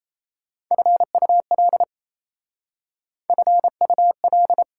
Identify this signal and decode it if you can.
Morse code, heard as a keyed tone, about 35 wpm, 700 Hz. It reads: FUL FUL